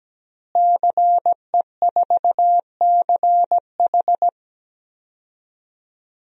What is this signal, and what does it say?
Morse code, heard as a keyed tone, about 17 wpm, 700 Hz. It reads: CE4CH
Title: DXCC-100 Call Signs - 5 Characters - Repeated 2x - 17wpm